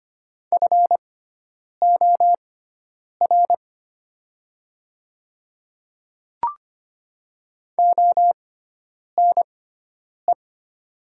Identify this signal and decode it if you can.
Morse code, heard as a keyed tone, about 25 wpm, spaced as for 9 wpm, 700 Hz. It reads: FOR ONE
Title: 6x Character Spacing - Top 100 Words - 25wpm